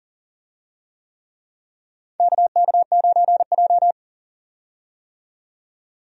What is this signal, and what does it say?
Morse code, heard as a keyed tone, about 40 wpm, 700 Hz. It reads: KK9J
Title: USA Call Signs - 4 Characters - 40wpm